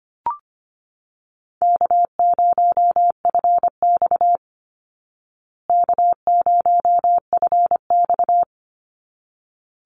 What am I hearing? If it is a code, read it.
Morse code, heard as a keyed tone, about 25 wpm, 700 Hz. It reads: K0FX K0FX
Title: Easy Call Signs Repeated 2x - 25wpm